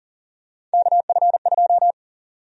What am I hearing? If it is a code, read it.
Morse code, heard as a keyed tone, about 40 wpm, 700 Hz. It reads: KF2
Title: USA Call Sign Prefixes - 40wpm